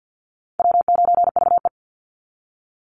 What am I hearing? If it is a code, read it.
Morse code, heard as a keyed tone, about 50 wpm, 700 Hz. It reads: W9VE
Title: Call Signs Repeated 2x - 50wpm